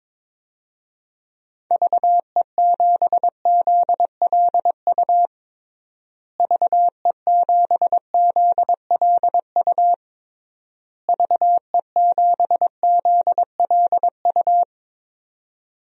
Morse code, heard as a keyed tone, about 22 wpm, 700 Hz. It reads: VE7ZLU VE7ZLU VE7ZLU
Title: DXCC-100 Call Signs - 6 Characters - Repeated 3x - 22wpm